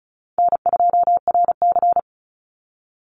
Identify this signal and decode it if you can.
Morse code, heard as a keyed tone, about 34 wpm, 700 Hz. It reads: N2RC
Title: Easy Call Signs Repeated 3x - 35wpm